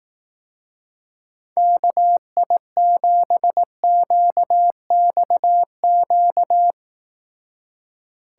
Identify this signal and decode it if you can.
Morse code, heard as a keyed tone, about 18 wpm, 700 Hz. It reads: KI7QXQ
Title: Speed Racing - USA 2x3 Call Signs 15wpm